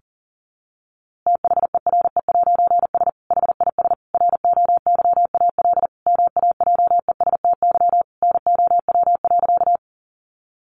Morse code, heard as a keyed tone, about 40 wpm, 700 Hz. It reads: THERE'S HIS ROYAL MAJESTY NOW.
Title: A Sound Of Thunder 40wpm